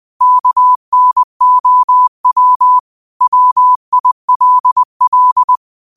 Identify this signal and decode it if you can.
Morse code, heard as a keyed tone, about 20 wpm, 1000 Hz. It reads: KNOW WILL
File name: 20wpm